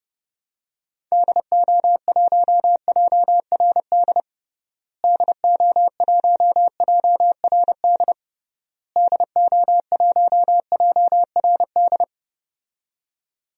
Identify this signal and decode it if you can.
Morse code, heard as a keyed tone, about 30 wpm, 700 Hz. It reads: DO1JRD DO1JRD DO1JRD
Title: DXCC-100 Call Signs - 6 Characters - Repeated 3x - 30wpm